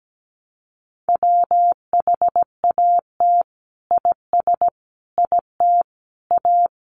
Morse code, heard as a keyed tone, about 17 wpm, 700 Hz. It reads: WHAT IS IT A
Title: Sentences from Top 1300 Words 17wpm